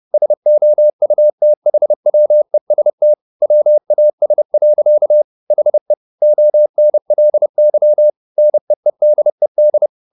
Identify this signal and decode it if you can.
Morse code, heard as a keyed tone, about 30 wpm, 600 Hz. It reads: SOUTHWEST WAS. HE ONLY NEEDED